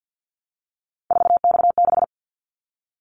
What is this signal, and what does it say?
Morse code, heard as a keyed tone, about 50 wpm, 700 Hz. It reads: 4X6
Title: Callsign Prefixes for Top 50 DXCC - 50wpm